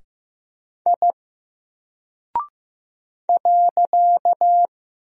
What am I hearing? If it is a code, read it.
Morse code, heard as a keyed tone, about 15 wpm, 700 Hz. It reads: I .
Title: Single Character (Letter:Number:Punctuation:Prosign) Alt 15wpm